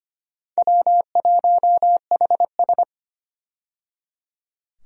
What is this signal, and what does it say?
Morse code, heard as a keyed tone, about 25 wpm, 700 Hz. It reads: W1HS
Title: USA Call Signs - 4 Characters - 25wpm